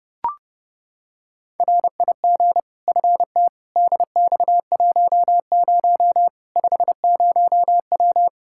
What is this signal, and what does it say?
Morse code, heard as a keyed tone, about 31 wpm, 700 Hz. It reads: RIG FT DX10 50W
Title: Practice QSO - Exchange 2 - Info - 30wpm